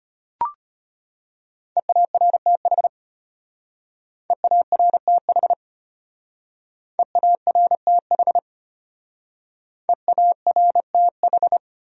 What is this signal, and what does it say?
Morse code, heard as a keyed tone, about 33 wpm, 700 Hz. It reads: EARTH EARTH EARTH EARTH